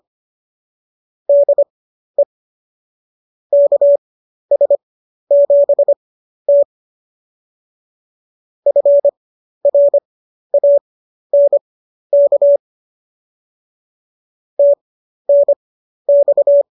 Morse code, heard as a keyed tone, about 25 wpm, spaced as for 12 wpm, 570 Hz. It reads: DE KS7T FRANK TNX